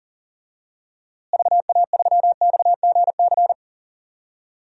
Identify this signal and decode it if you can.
Morse code, heard as a keyed tone, about 40 wpm, 700 Hz. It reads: VA3XGC
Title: DXCC-100 Call Signs - 6 Characters - Repeated 3x - 40wpm